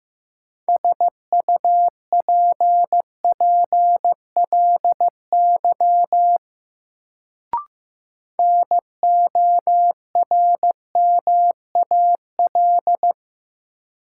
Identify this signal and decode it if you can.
Morse code, heard as a keyed tone, about 15 wpm, 700 Hz. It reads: SUPPLY NORMAL
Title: Sentences from Top 1200 Words - New Word Drill Alt - 15wpm